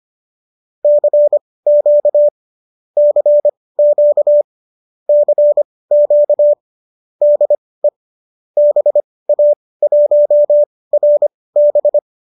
Morse code, heard as a keyed tone, about 25 wpm, spaced as for 18 wpm, 590 Hz. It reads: CQ CQ CQ DE BA1RB